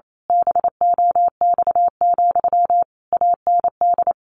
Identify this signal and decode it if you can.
Morse code, heard as a keyed tone, about 28 wpm, 700 Hz. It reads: BOX, AND
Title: A Sound Of Thunder 28wpm